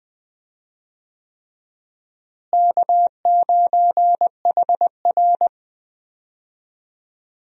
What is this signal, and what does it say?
Morse code, heard as a keyed tone, about 20 wpm, 700 Hz. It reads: K9HR
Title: USA Call Signs - 4 Characters - Repeated 3x - 20wpm